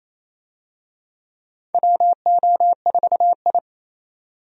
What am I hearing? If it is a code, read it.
Morse code, heard as a keyed tone, about 27 wpm, 700 Hz. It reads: WO4I
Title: USA Call Signs - 4 Characters - 28wpm